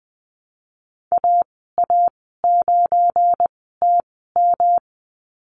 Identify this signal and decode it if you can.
Morse code, heard as a keyed tone, about 20 wpm, spaced as for 15 wpm, 700 Hz. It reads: AA9TM
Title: 173 Lesson 28g - Number 9 - Call Signs 20wpm